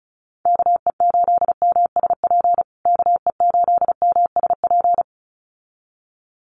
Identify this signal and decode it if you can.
Morse code, heard as a keyed tone, about 34 wpm, 700 Hz. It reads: KE8MSP KE8MSP
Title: USA Call Signs Repeated 2x 35wpm